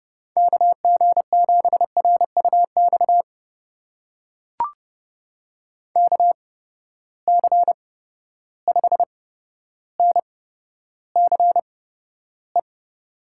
Morse code, heard as a keyed tone, about 31 wpm, spaced as for 13 wpm, 700 Hz. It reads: KG7RUX KC5NCE
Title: USA Call Signs - 6 Characters - 8x Character Spacing - 30wpm